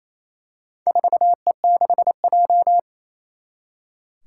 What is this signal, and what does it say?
Morse code, heard as a keyed tone, about 28 wpm, 700 Hz. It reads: 4E6J
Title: Four Challenging Letter-Number Combinations 28wpm